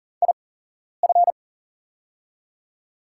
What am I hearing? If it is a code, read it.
Morse code, heard as a keyed tone, about 40 wpm, spaced as for 11 wpm, 700 Hz. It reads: IF
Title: 8x Character Spacing - Top 200 Words - 40wpm